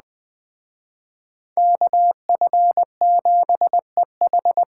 Morse code, heard as a keyed tone, about 20 wpm, 700 Hz. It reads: KF7EH
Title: Speed Racing - USA 2x2 Call Signs 17wpm